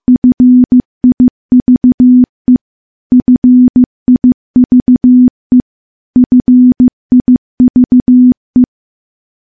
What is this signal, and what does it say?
Morse code, heard as a keyed tone, about 15 wpm, 250 Hz. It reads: FIVE FIVE FIVE
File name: morse-training-15WPM